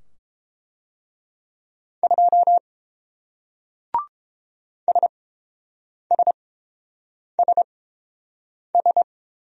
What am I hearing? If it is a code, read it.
Morse code, heard as a keyed tone, about 30 wpm, spaced as for 7 wpm, 700 Hz. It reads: 2 SSSS